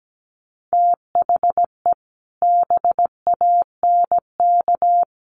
Morse code, heard as a keyed tone, about 17 wpm, 700 Hz. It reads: THE BANK
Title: Sentences from Top 1500 Words 17wpm